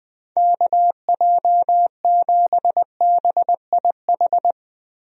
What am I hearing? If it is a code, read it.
Morse code, heard as a keyed tone, about 20 wpm, 700 Hz. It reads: KJ7BIH
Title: USA Call Signs - 6 Characters - Repeated 3x - 20wpm